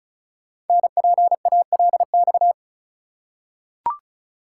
Morse code, heard as a keyed tone, about 36 wpm, 700 Hz. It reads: NPALX
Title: Code Groups - 5 Letter:Numbers - 35wpm